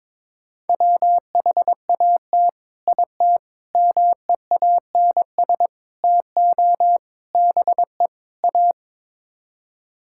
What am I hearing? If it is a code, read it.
Morse code, heard as a keyed tone, about 21 wpm, 700 Hz. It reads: WHAT IT MEANS TO BE A
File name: practice / 22wpm / Sets of 6 Words - 22wpm